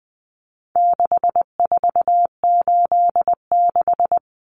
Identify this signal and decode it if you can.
Morse code, heard as a keyed tone, about 20 wpm, 700 Hz. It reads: 6486
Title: Numbers - 4-Digit 20wpm